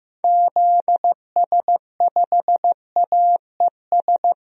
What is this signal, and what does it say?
Morse code, heard as a keyed tone, about 15 wpm, 700 Hz. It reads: ZS5AES
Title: DXCC-100 Call Signs - 6 Characters - Repeated 3x - 15wpm